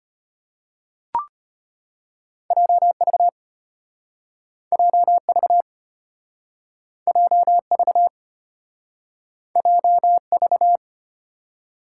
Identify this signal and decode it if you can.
Morse code, heard as a keyed tone, about 35 wpm, 700 Hz. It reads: JV JV JV JV